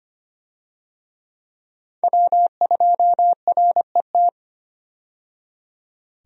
Morse code, heard as a keyed tone, about 25 wpm, 700 Hz. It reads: W2RET
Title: USA Call Signs - 5 Characters - 25wpm